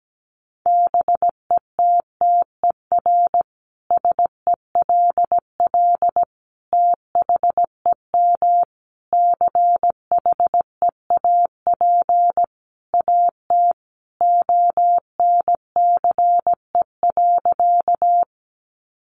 Morse code, heard as a keyed tone, about 17 wpm, 700 Hz. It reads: BETTER SELL THEM CHEAP AT ONCE.
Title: Sentences from Top 1200 Words 17wpm